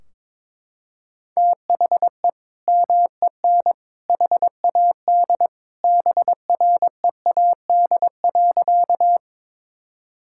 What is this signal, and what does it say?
Morse code, heard as a keyed tone, about 22 wpm, 700 Hz. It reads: THE MEN HAD BREAD.
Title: Sentences from Top 1600 Words 22wpm